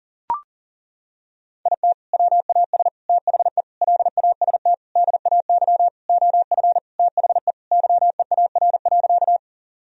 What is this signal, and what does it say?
Morse code, heard as a keyed tone, about 40 wpm, 700 Hz. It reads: IT WAS THE LAST DAY OF THE YEAR.
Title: Sentences from Top 1000 Words - Review Part 1 - 40wpm